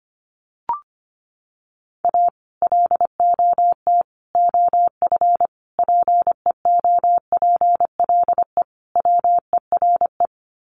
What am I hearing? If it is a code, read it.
Morse code, heard as a keyed tone, about 25 wpm, 700 Hz. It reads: A LOT OF PEOPLE WERE
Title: Sets of 5 Words Encore2 25wpm